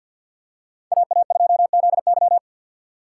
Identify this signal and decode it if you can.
Morse code, heard as a keyed tone, about 52 wpm, 700 Hz. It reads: AA2ZY
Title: USA Call Signs - 5 Characters - 2x Character Spacing - 50wpm